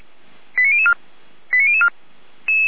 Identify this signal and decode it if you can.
Five-tone selective call (ZVEI), heard as quick runs of five tones, tones of about 80 ms. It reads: 89004 89004
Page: ZVEI